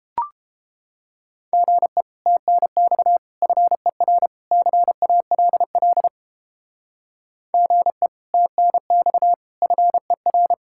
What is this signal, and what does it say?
Morse code, heard as a keyed tone, about 29 wpm, 700 Hz. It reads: GE TNX FER CALL GE TNX FER